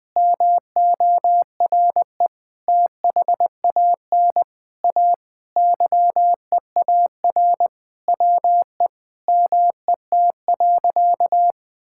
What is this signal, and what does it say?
Morse code, heard as a keyed tone, about 20 wpm, 700 Hz. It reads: MORE THAN A YEAR WE MET.
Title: Sentences with 7-Word Phrases - 20wpm